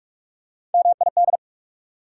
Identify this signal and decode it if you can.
Morse code, heard as a keyed tone, about 44 wpm, 690 Hz. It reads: MID
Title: Prefixes and Suffixes 45wpm